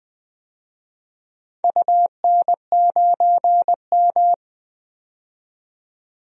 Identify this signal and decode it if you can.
Morse code, heard as a keyed tone, about 20 wpm, 700 Hz. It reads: UN9M
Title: DXCC-100 Call Signs - 4 Characters - 20wpm